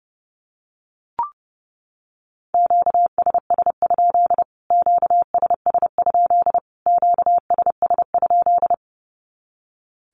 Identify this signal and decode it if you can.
Morse code, heard as a keyed tone, about 30 wpm, 700 Hz. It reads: QSS? QSS? QSS?
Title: Q-Codes Repeated 3x 30wpm